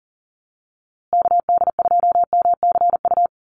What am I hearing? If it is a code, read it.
Morse code, heard as a keyed tone, about 40 wpm, 700 Hz. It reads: KD2MCU
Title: USA Call Signs Repeated 2x 40wpm